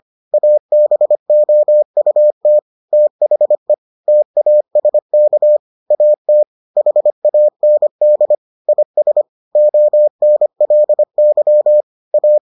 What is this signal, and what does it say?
Morse code, heard as a keyed tone, about 25 wpm, 600 Hz. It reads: ABOUT THE TASK AT HAND IS ONLY A